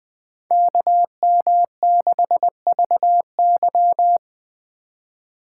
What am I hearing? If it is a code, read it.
Morse code, heard as a keyed tone, about 24 wpm, 700 Hz. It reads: KM6VY
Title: Speed Racing - USA Call Signs 20wpm